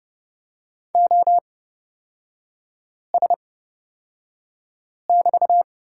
Morse code, heard as a keyed tone, about 30 wpm, spaced as for 5 wpm, 700 Hz. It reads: OS<BT>
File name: Lesson 36e - BT - Three Characters 30wpm